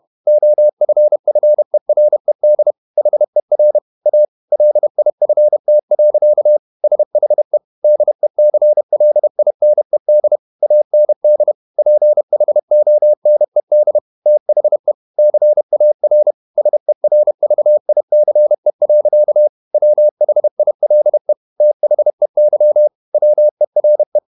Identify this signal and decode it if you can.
Morse code, heard as a keyed tone, about 31 wpm, 600 Hz. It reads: OFFERED HER A LIFT. SHE DECLINED AND PHONED THE CAR SERVICE. WHILE THEY WERE
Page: Never 31wpm | CW med Gnister